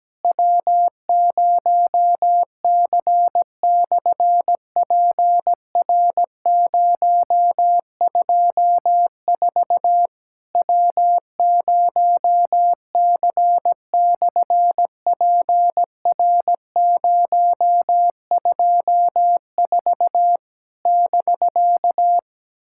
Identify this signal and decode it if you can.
Morse code, heard as a keyed tone, about 17 wpm, 700 Hz. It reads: W0C/PR024 W0C/PR024 <BK>
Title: SOTA QSO Practice 17wpm